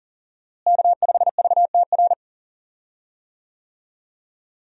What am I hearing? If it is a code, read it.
Morse code, heard as a keyed tone, about 40 wpm, 700 Hz. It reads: K5VTR
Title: USA Call Signs - 5 Characters - Repeated 2x - 40wpm